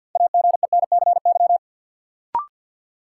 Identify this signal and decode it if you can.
Morse code, heard as a keyed tone, about 51 wpm, 700 Hz. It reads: AGENCY
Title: Sentences from Top 1300 Words - New Word Drill Alt - 50wpm